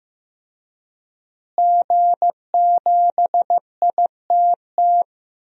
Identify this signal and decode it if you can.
Morse code, heard as a keyed tone, about 15 wpm, 700 Hz. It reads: G7ITT
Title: DXCC-100 Call Signs - 5 Characters - Repeated 2x - 15wpm